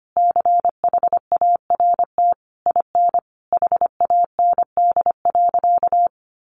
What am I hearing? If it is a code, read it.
Morse code, heard as a keyed tone, about 25 wpm, 700 Hz. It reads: CHART IN HAND.
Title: Sentences from Top 1300 Words 25wpm